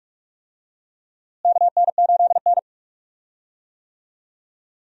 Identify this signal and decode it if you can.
Morse code, heard as a keyed tone, about 47 wpm, 700 Hz. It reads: KN8N
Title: USA Call Signs - 4 Characters - Repeated 2x - 45wpm